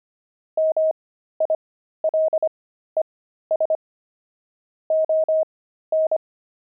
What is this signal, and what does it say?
Morse code, heard as a keyed tone, about 25 wpm, spaced as for 13 wpm, 630 Hz. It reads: MILES ON
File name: pangram-fs-13-cs-25-pavement